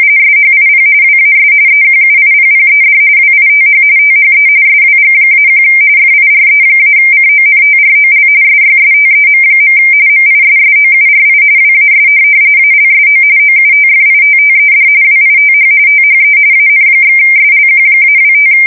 RTTY, about 45.45 baud. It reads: SHIP WAS EQUIPPED WITH THE SWG-1 FIRE-CONTROL SYSTEM. MISSOURI WAS ALSO OUTFITTED WITH THE AN/SLQ-25
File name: fsk170 / USS-Missouri